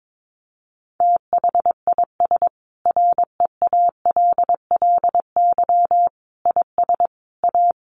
Morse code, heard as a keyed tone, about 22 wpm, 700 Hz. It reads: THIS REALLY IS A
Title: Sentences from Top 700 Words 22wpm